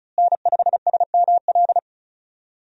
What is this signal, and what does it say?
Morse code, heard as a keyed tone, about 34 wpm, 700 Hz. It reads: N5SML
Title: USA Call Signs - 5 Characters - Repeated 2x - 35wpm